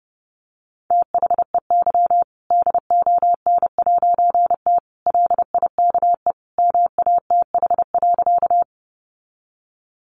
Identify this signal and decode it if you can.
Morse code, heard as a keyed tone, about 30 wpm, 700 Hz. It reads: THEY DON'T LIKE MATH.
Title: Easy Phrases 30wpm